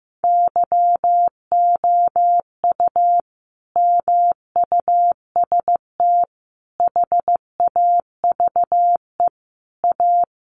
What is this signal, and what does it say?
Morse code, heard as a keyed tone, about 15 wpm, 700 Hz. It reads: YOU MUST HAVE A
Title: Sentences from Top 500 Words 15wpm